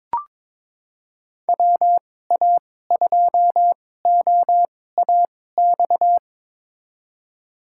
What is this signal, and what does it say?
Morse code, heard as a keyed tone, about 22 wpm, spaced as for 16 wpm, 700 Hz. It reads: WA2OAX
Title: USA Call Signs - 6 Characters - 2x Character Spacing - 22wpm